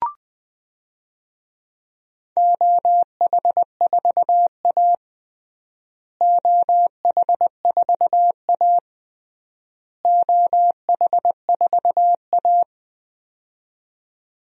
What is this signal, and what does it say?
Morse code, heard as a keyed tone, about 20 wpm, 700 Hz. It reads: OH4A OH4A OH4A
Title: DXCC-100 Call Signs - 4 Characters - Repeated 3x - 20wpm